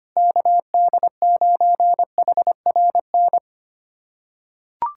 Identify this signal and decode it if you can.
Morse code, heard as a keyed tone, about 25 wpm, 700 Hz. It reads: KD9HRN
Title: USA Callsign DE Callsign 25wpm